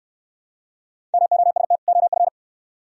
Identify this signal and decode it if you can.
Morse code, heard as a keyed tone, about 63 wpm, 700 Hz. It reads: MOST OF